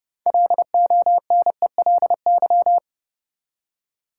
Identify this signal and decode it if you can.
Morse code, heard as a keyed tone, about 31 wpm, 700 Hz. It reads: LONELY
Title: Six Letter Words - 30wpm